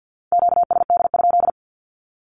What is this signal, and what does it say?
Morse code, heard as a keyed tone, about 48 wpm, 700 Hz. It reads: QSD?
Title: Q-Codes 50wpm